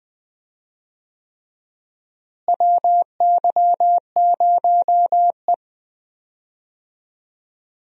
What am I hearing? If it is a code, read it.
Morse code, heard as a keyed tone, about 20 wpm, 700 Hz. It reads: WY0E
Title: USA Call Signs - 4 Characters - Repeated 3x - 20wpm